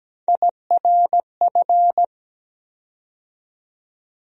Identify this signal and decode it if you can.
Morse code, heard as a keyed tone, about 17 wpm, 700 Hz. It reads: IRF
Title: USA Call Sign Postfixes - 3 Letters 17wpm